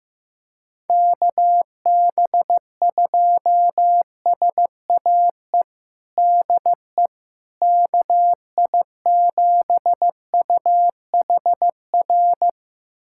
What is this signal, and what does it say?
Morse code, heard as a keyed tone, about 15 wpm, 700 Hz. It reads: KB2SAE DE KI7UHR
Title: USA Callsign DE Callsign - 6 Characters - 15wpm